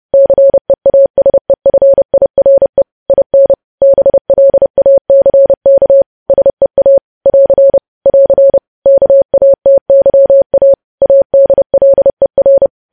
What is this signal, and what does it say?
Morse code, heard as a keyed tone, about 30 wpm, 550 Hz. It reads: CEASEFIRE IN BLACK SEA <AR> <AR> KATYA ADLER
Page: news headlines in morse code at 30 wpm